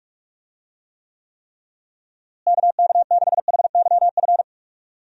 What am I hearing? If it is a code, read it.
Morse code, heard as a keyed tone, about 45 wpm, 700 Hz. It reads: KK6HYF
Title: USA Call Signs - 6 Characters - Repeated 3x - 45wpm